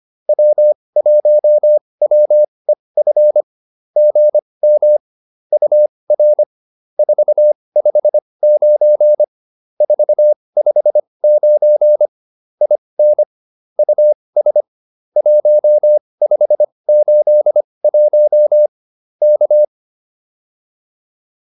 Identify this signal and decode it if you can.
Morse code, heard as a keyed tone, about 25 wpm, spaced as for 20 wpm, 600 Hz. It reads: W1WEF GM UR 459 459 IN US 1581 K